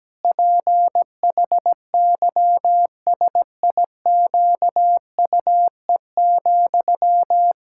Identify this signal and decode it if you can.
Morse code, heard as a keyed tone, about 17 wpm, 700 Hz. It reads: PHYSIQUE,
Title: Pangram Challenge v2 - 17wpm